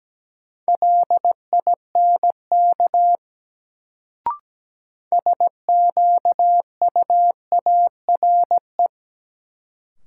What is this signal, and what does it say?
Morse code, heard as a keyed tone, about 17 wpm, 700 Hz. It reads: LINK SQUARE
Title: Sentences from Top 1000 Words - Review Part 7 - New Word Drill - 17wpm